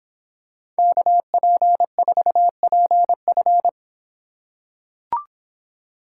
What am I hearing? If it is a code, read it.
Morse code, heard as a keyed tone, about 25 wpm, 700 Hz. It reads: KP4PF